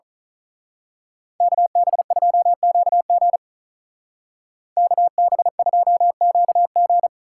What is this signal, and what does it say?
Morse code, heard as a keyed tone, about 39 wpm, 700 Hz. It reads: KB2QG KB2QG
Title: Speed Racing - USA 2x2 Call Signs 35wpm